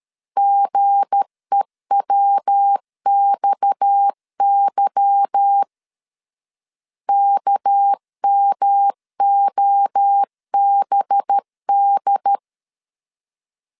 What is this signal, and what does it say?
Morse code, heard as a keyed tone, about 13 wpm, 800 Hz. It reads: GEWXY KMOBD
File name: LEZIONI / 8 YJ45 / 8 YJ 12wpm 2235